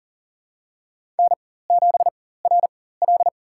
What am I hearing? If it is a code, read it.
Morse code, heard as a keyed tone, about 40 wpm, spaced as for 19 wpm, 700 Hz. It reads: N7RL
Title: USA Call Signs - 4x Character Spacing - 40wpm